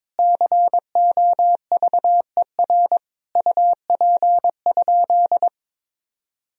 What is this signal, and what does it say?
Morse code, heard as a keyed tone, about 22 wpm, 700 Hz. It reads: COVER UP?
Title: Sentences from Top 1000 Words - Review Part 1 - 22wpm